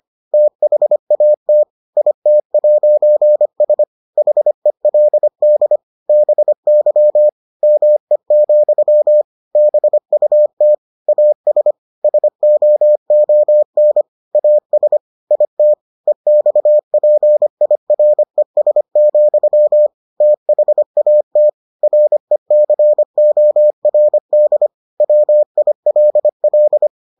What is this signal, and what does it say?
Morse code, heard as a keyed tone, about 25 wpm, 600 Hz. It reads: THAT IT'S HELD BY ME, BUT AS SOON AS IT EXPIRES, THAT RECORD WILL